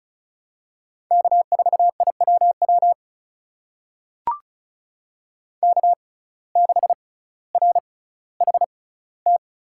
Morse code, heard as a keyed tone, about 36 wpm, spaced as for 18 wpm, 700 Hz. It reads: K4IWW K6RHT
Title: USA Call Signs - 5 Characters - 6x Character Spacing - 35wpm